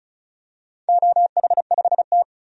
Morse code, heard as a keyed tone, about 36 wpm, 700 Hz. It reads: OH5T
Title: DXCC-100 Call Signs - 4 Characters - 35wpm